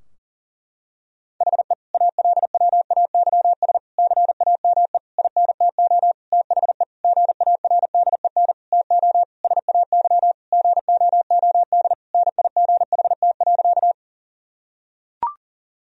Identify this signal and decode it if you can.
Morse code, heard as a keyed tone, about 40 wpm, 700 Hz. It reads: HE ALWAYS CAME INTO THE GARDEN TO SAY GOOD NIGHT.
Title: Sentences from Top 1000 Words - Review Part 4 - 40wpm